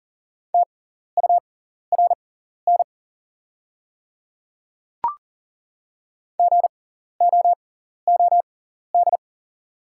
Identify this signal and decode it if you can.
Morse code, heard as a keyed tone, about 40 wpm, spaced as for 14 wpm, 700 Hz. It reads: TURN GOOD
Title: 6x Character Spacing - Top 200 Words - 40wpm